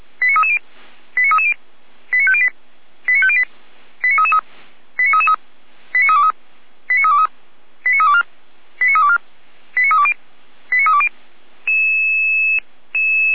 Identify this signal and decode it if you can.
Five-tone selective call (ZVEI), heard as quick runs of five tones, tones of about 70 ms. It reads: 89330 89330 89508 89508 89333 89333 89323 89323 89324 89324 89320 89320
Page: ZVEI